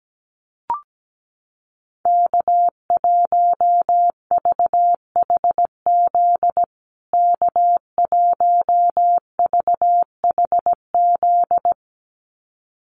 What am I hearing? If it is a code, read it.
Morse code, heard as a keyed tone, about 17 wpm, 700 Hz. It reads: K1VHZ K1VHZ
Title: USA Call Signs Repeated 2x 17wpm